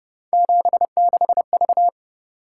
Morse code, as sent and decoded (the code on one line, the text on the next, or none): --... -.... ...-
76V